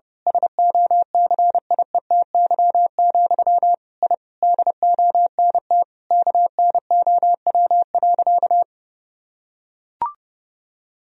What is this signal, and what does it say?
Morse code, heard as a keyed tone, about 30 wpm, 700 Hz. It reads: SOCIETY, I DONT KNOW.
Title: Sentences from Top 1000 Words - Review Part 4 - 30wpm